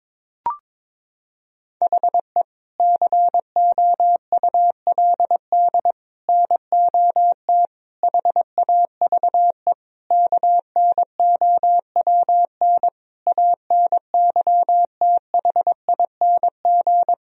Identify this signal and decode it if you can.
Morse code, heard as a keyed tone, about 22 wpm, 700 Hz. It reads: HE COULD NOT HAVE KNOWN ANYTHING
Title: Sentences from Top 1000 Words - Review Part 3 - 22wpm